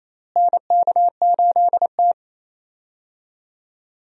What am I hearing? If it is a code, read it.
Morse code, heard as a keyed tone, about 28 wpm, 700 Hz. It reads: NK8T
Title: USA Call Signs - 4 Characters - Repeated 3x - 28wpm